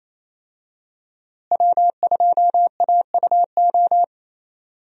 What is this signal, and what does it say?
Morse code, heard as a keyed tone, about 27 wpm, 700 Hz. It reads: W2AUO
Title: USA Call Signs - 5 Characters - Repeated 3x - 28wpm